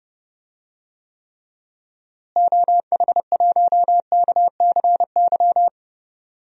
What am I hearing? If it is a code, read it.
Morse code, heard as a keyed tone, about 31 wpm, 700 Hz. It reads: OH1KCY
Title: DXCC-100 Call Signs - 6 Characters - Repeated 3x - 30wpm